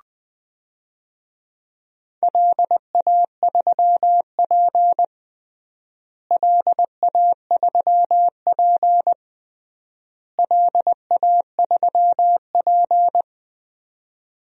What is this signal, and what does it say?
Morse code, heard as a keyed tone, about 20 wpm, 700 Hz. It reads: LA3P LA3P LA3P
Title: DXCC-100 Call Signs - 4 Characters - Repeated 3x - 20wpm